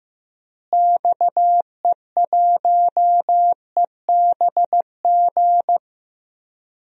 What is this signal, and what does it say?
Morse code, heard as a keyed tone, about 16 wpm, 700 Hz. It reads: XE1EBG
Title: DXCC-100 Call Signs - 6 Characters - Repeated 2x - 15wpm